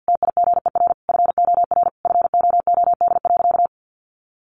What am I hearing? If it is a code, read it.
Morse code, heard as a keyed tone, about 48 wpm, 700 Hz. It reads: TIGER FOR FOOD.
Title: A Sound Of Thunder 50wpm